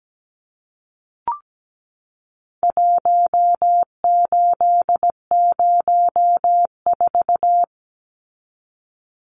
Morse code, heard as a keyed tone, about 17 wpm, 700 Hz. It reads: 1804
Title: Numbers - Contest Style - 4-Digit 17wpm